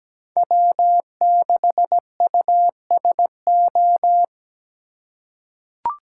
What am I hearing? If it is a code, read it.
Morse code, heard as a keyed tone, about 17 wpm, 700 Hz. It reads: W6USO
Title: USA Call Signs - 5 Characters - 3x Character Spacing - 17wpm